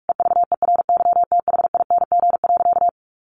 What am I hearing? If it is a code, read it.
Morse code, heard as a keyed tone, about 44 wpm, 700 Hz. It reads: EVERYTHING.
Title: Sentences from Top 1200 Words 45wpm